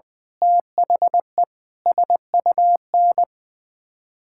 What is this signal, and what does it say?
Morse code, heard as a keyed tone, about 20 wpm, 700 Hz. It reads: THE SUN
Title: Sentences with 7-Word Phrases - Spoken Beginning - 20wpm